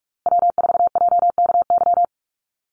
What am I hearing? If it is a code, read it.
Morse code, heard as a keyed tone, about 44 wpm, 700 Hz. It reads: W4JKY
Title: USA Call Signs Repeated 2x 45wpm